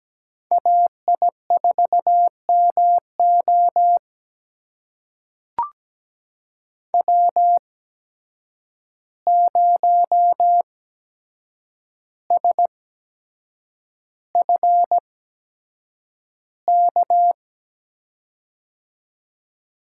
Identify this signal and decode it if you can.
Morse code, heard as a keyed tone, about 17 wpm, spaced as for 7 wpm, 700 Hz. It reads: AI4MO W0SFK
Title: USA Call Signs - 5 Characters - 8x Character Spacing - 17wpm